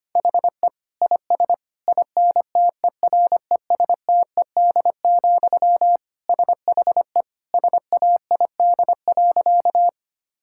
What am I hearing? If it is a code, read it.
Morse code, heard as a keyed tone, about 25 wpm, 700 Hz. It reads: HE IS INTERESTED, SHE SAID.
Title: Sentences from Top 1000 Words - Review Part 6 - 25wpm